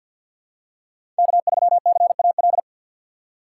Morse code, heard as a keyed tone, about 51 wpm, 700 Hz. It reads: K3CAL
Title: USA Call Signs - 5 Characters - 6x Character Spacing - 50wpm